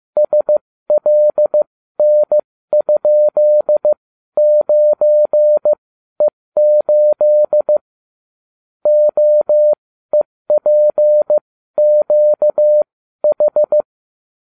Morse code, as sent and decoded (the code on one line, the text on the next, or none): ... .-.. -. ..--.. ----. ---.. / --- . .--. --.- ....
SLN?9E8 OEPQH